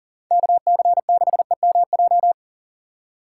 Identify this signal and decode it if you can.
Morse code, heard as a keyed tone, about 50 wpm, 700 Hz. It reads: KC6EMJ
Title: Speed Racing - USA 2x3 Call Signs 40wpm